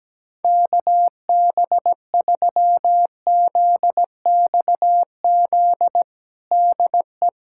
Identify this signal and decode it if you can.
Morse code, heard as a keyed tone, about 17 wpm, 700 Hz. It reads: KB3ZXZ DE
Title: USA Callsign DE Callsign - 6 Characters - 17wpm